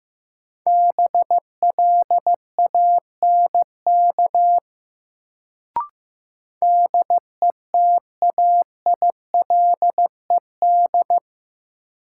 Morse code, heard as a keyed tone, about 15 wpm, 700 Hz. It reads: BLANK DETAILED
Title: Sentences from Top 1500 Words - New Word Drill Alt - 15wpm